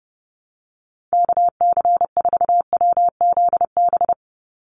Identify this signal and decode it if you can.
Morse code, heard as a keyed tone, about 31 wpm, 700 Hz. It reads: KC4WZB
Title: NA SOTA Activator Callsigns 30wpm